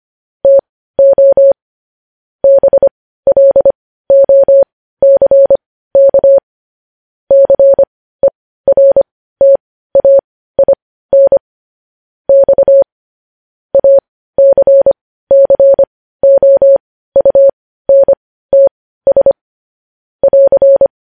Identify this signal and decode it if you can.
Morse code, heard as a keyed tone, about 25 wpm, spaced as for 15 wpm, 550 Hz. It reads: TO BLOCK CERTAIN X ACCOUNTS <AR>